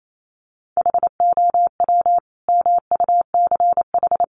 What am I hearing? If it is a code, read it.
Morse code, heard as a keyed tone, about 28 wpm, 700 Hz. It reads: HOW MUCH
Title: Sentences from Top 1400 Words 28wpm